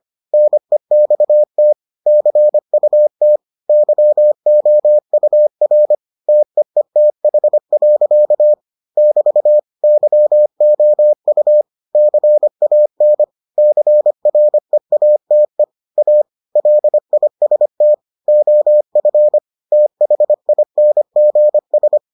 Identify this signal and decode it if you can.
Morse code, heard as a keyed tone, about 25 wpm, 600 Hz. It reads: NEXT CUT YOUR TEETH. <BT> YOU CAN CREATE A LIST OF THINGS